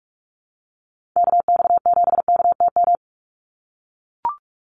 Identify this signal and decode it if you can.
Morse code, heard as a keyed tone, about 44 wpm, 700 Hz. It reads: KX7KTM
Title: USA Call Signs 45wpm